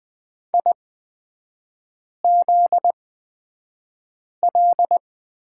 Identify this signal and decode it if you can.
Morse code, heard as a keyed tone, about 20 wpm, spaced as for 5 wpm, 700 Hz. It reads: IZL
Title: 243 Lesson 39e - Letter z - Three Characters 20wpm